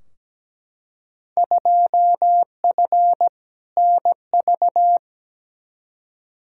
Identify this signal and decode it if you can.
Morse code, heard as a keyed tone, about 17 wpm, 700 Hz. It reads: 2F NV
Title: Field Day Exchanges - 17wpm